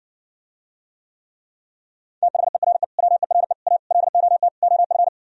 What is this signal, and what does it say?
Morse code, heard as a keyed tone, about 60 wpm, 700 Hz. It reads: THERE WERE A LOT OF